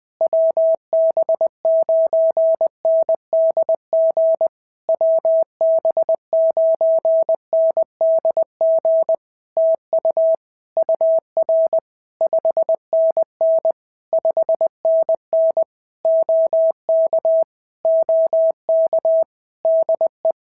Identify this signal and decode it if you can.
Morse code, heard as a keyed tone, about 20 wpm, 650 Hz. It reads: WB9NDG WB9NDG TU UR 5NN 5NN OK OK DE
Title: POTA QSO Practice - 20wpm